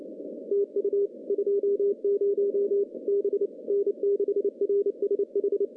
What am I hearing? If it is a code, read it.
Morse code, heard as a keyed tone, about 28 wpm, 400 Hz. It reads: TU 20 B N6RSH
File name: CW-SS-k5tr